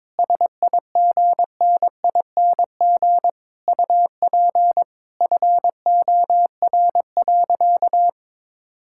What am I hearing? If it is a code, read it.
Morse code, heard as a keyed tone, about 22 wpm, 700 Hz. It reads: SIGNING UP FOR.
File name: Sentences with 7-Word Phrases - 22wpm